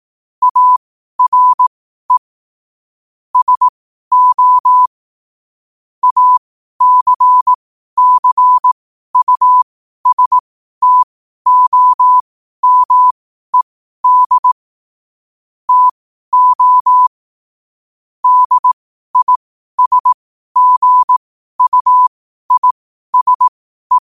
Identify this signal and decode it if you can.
Morse code, heard as a keyed tone, about 18 wpm, spaced as for 12 wpm, 1000 Hz. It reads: ARE SO ACCUSTOMED TO DISGUISE
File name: qotdfiles / 12wpm